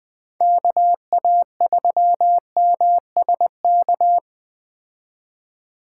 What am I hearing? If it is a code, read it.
Morse code, heard as a keyed tone, about 20 wpm, 700 Hz. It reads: KA3MSK